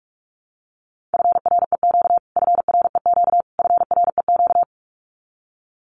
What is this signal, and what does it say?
Morse code, heard as a keyed tone, about 44 wpm, 700 Hz. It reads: FREQ FREQ FREQ
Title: Abbreviations Repeated 3x - 45wpm